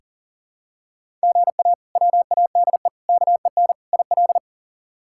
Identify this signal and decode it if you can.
Morse code, heard as a keyed tone, about 40 wpm, 700 Hz. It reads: GA WADE KEN IL
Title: K1USN SST QSO Practice - 40wpm